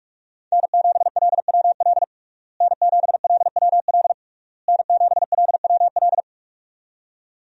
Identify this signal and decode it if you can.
Morse code, heard as a keyed tone, about 43 wpm, 700 Hz. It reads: N7LWL N7LWL N7LWL
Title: USA Call Signs - 5 Characters - Repeated 3x - 45wpm